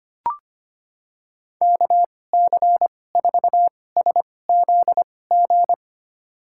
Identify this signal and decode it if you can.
Morse code, heard as a keyed tone, about 25 wpm, spaced as for 18 wpm, 700 Hz. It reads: KC4SZG
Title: USA Call Signs - 6 Characters - 2x Character Spacing - 25wpm